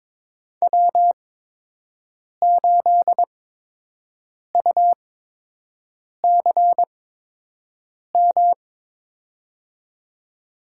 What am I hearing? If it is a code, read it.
Morse code, heard as a keyed tone, about 22 wpm, spaced as for 6 wpm, 700 Hz. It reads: W8UCM
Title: USA Call Signs - 8x Character Spacing - 22wpm